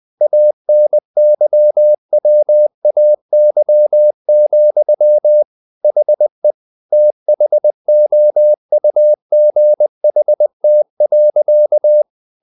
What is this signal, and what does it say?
Morse code, heard as a keyed tone, about 20 wpm, 600 Hz. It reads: ANYWAY, HE THOUGHT.
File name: the old man and the sea / the old man and the sea part 1 20wpm